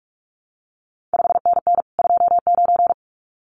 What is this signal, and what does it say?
Morse code, heard as a keyed tone, about 46 wpm, 700 Hz. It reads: 5NN 29
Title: CQ WW DX Contest Exchange Encore - 45wpm